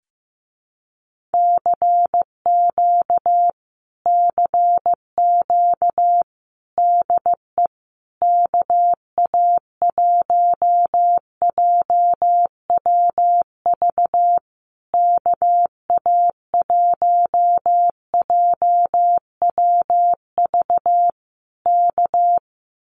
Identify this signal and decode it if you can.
Morse code, heard as a keyed tone, about 15 wpm, 700 Hz. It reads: CQ CQ DE KA1JWV KA1JWV K
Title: Common QSO Phrases Encore 15wpm